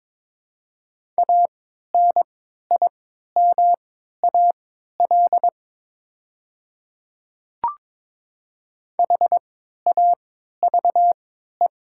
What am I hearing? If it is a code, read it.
Morse code, heard as a keyed tone, about 22 wpm, spaced as for 12 wpm, 700 Hz. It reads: ANIMAL HAVE
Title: 3x Character Spacing - Top 300 Words - 22wpm